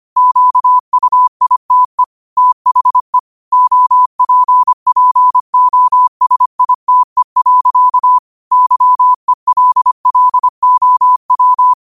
Morse code, heard as a keyed tone, about 25 wpm, 1000 Hz. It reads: QUITE THE OPPOSITE. YELLOW